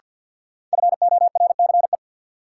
Morse code, heard as a keyed tone, about 50 wpm, 700 Hz. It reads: FORCE
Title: Sentences from Top 700 Words - New Word Drill - 50wpm